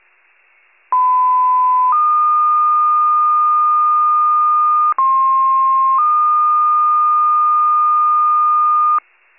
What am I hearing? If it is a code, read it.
Two-tone page, heard as a one-second tone, then a three-second tone, about 1010 then 1220 Hz, about 1060 then 1220 Hz. It is heard from 0.9 to 9.0 s.